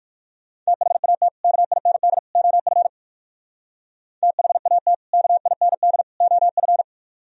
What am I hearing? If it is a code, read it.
Morse code, heard as a keyed tone, about 56 wpm, 700 Hz. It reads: THAT KIND OF THAT KIND OF